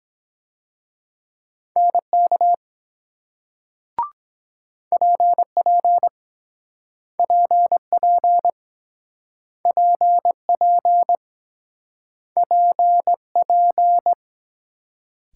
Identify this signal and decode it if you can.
Morse code, heard as a keyed tone, about 23 wpm, 700 Hz. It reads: NK PP PP PP PP